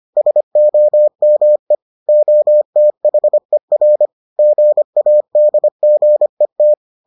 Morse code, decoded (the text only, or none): SOME OTHER GADGET